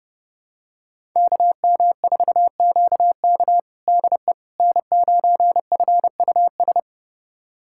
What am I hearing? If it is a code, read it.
Morse code, heard as a keyed tone, about 32 wpm, 700 Hz. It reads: KM4QK DE N9FUS
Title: USA Callsign DE Callsign - 5 Characters - 30wpm